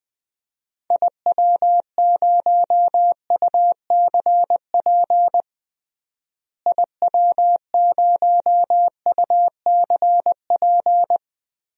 Morse code, heard as a keyed tone, about 20 wpm, 700 Hz. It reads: IW0UCP IW0UCP
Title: DXCC-100 Call Signs - 6 Characters - Repeated 3x - 20wpm